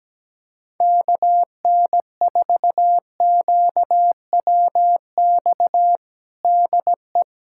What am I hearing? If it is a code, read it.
Morse code, heard as a keyed tone, about 17 wpm, 700 Hz. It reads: KN4QWX DE
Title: USA Callsign DE Callsign - 6 Characters - 17wpm